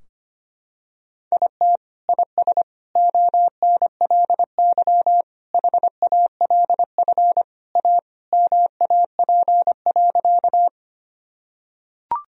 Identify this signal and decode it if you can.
Morse code, heard as a keyed tone, about 25 wpm, 700 Hz. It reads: IT IS ONLY HALF A MAP.
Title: Sentences from Top 1000 Words - Review Part 2 - 25wpm